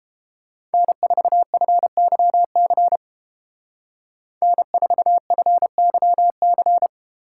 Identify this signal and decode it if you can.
Morse code, heard as a keyed tone, about 29 wpm, 700 Hz. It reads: N4FYC N4FYC